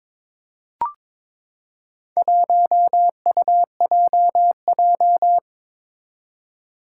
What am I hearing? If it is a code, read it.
Morse code, heard as a keyed tone, about 22 wpm, 700 Hz. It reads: 1UJJ
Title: Four Challenging Letter-Number Combinations 22wpm